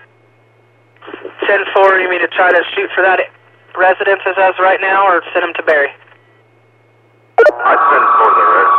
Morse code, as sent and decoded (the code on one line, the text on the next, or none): none